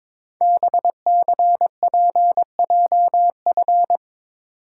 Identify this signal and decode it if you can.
Morse code, heard as a keyed tone, about 22 wpm, 700 Hz. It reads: BCPJF
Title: Code Groups - 5 Letters - 22wpm